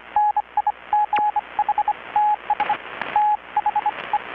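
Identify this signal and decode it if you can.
Morse code, heard as a keyed tone, about 25 wpm, spaced as for 22 wpm, 840 Hz, with radio noise. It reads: NIGHTS THE
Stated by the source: Maritime radiotelegraphy coast station event transmission, received in Chiapas, Mexico about 3,800 km away on a Kenwood R-600 receiver with a 10-metre grounded longwire antenna.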